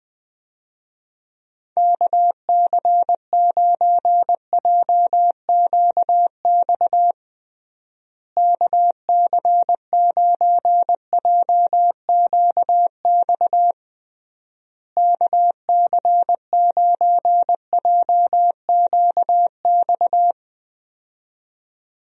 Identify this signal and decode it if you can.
Morse code, heard as a keyed tone, about 21 wpm, 700 Hz. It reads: KC9JQX KC9JQX KC9JQX
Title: USA Call Signs - 6 Characters - Repeated 3x - 20wpm